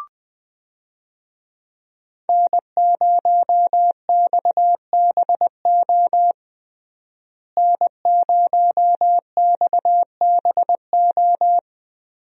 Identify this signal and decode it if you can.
Morse code, heard as a keyed tone, about 21 wpm, 700 Hz. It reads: N0XBO N0XBO
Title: USA Call Signs - 5 Characters - Repeated 2x - 20wpm